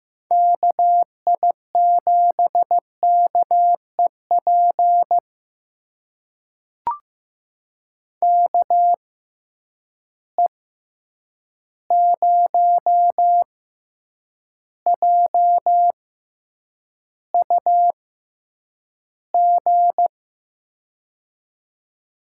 Morse code, heard as a keyed tone, about 15 wpm, spaced as for 8 wpm, 700 Hz. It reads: KI7KEP KE0JUG
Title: USA Call Signs - 6 Characters - 6x Character Spacing - 15wpm